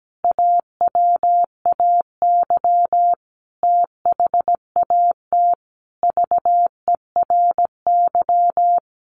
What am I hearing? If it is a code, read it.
Morse code, heard as a keyed tone, about 17 wpm, 700 Hz. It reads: AWAY THAT VERY
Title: Sentences from Top 1200 Words 17wpm